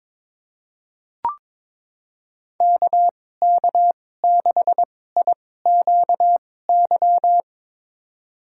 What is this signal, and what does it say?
Morse code, heard as a keyed tone, about 22 wpm, spaced as for 16 wpm, 700 Hz. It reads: KK6IQY
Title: USA Call Signs - 6 Characters - 2x Character Spacing - 22wpm